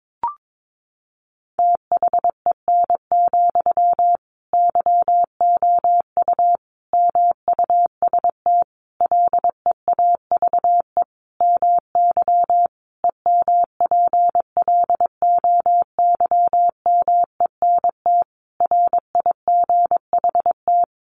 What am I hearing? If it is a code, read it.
Morse code, heard as a keyed tone, about 22 wpm, 700 Hz. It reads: THEN, YOU MUST LEAVE MY EMPLOYMENT RIGHT
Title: Sentences from Top 1500 Words 22wpm